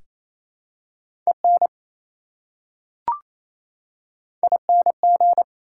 Morse code, heard as a keyed tone, about 28 wpm, 700 Hz. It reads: EN ING
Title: Prefixes and Suffixes 28wpm